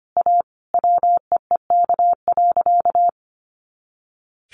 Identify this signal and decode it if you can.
Morse code, heard as a keyed tone, about 25 wpm, 700 Hz. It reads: A WEEK.
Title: Easy Phrases 25wpm